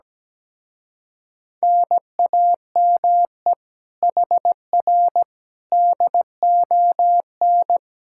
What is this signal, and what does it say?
Morse code, heard as a keyed tone, about 17 wpm, 700 Hz. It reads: NAME HR DON
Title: Speed-Racing Common QSO Phrases - 17wpm